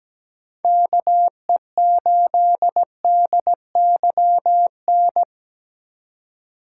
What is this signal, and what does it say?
Morse code, heard as a keyed tone, about 17 wpm, 700 Hz. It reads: KE8DYN
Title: USA Call Signs - 6 Characters - 4x Character Spacing - 17wpm